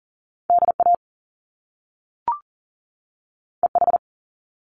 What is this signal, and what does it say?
Morse code, heard as a keyed tone, about 38 wpm, 700 Hz. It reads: DA EH
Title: Top Two Letter Combinations in Words 40wpm